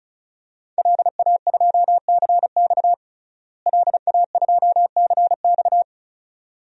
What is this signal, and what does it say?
Morse code, heard as a keyed tone, about 36 wpm, 700 Hz. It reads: LA2CX LA2CX
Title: DXCC-100 Call Signs - 5 Characters - Repeated 2x - 35wpm